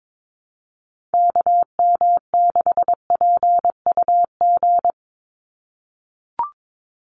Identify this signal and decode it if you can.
Morse code, heard as a keyed tone, about 22 wpm, 700 Hz. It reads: KM6PUG
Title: USA Call Signs 22wpm